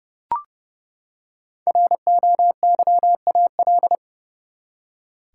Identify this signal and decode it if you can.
Morse code, heard as a keyed tone, about 30 wpm, 700 Hz. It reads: ROYAL
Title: Five Letter Words - 30wpm